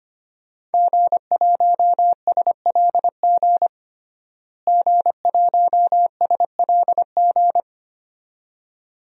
Morse code, heard as a keyed tone, about 25 wpm, 700 Hz. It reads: G1SLG G1SLG
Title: DXCC-100 Call Signs - 5 Characters - Repeated 2x - 25wpm